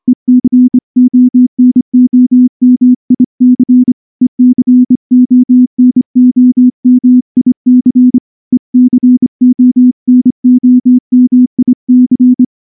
Morse code, heard as a keyed tone, about 25 wpm, 250 Hz. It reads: ECONOMIC ECONOMIC ECONOMIC
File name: morse / morse-training-25WPM